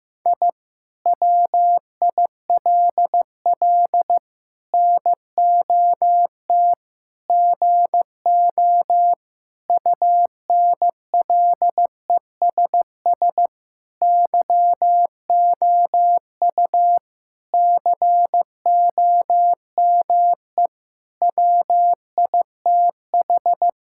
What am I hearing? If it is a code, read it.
Morse code, heard as a keyed tone, about 15 wpm, 700 Hz. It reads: I WILL NOT GO UNLESS YOU COME WITH
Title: Sentences from Top 1000 Words - Review Part 6 - 15wpm